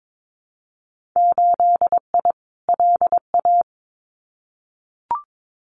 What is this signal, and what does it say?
Morse code, heard as a keyed tone, about 22 wpm, 700 Hz. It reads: 8I LA
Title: Winter Field Day Exchanges Encore 22wpm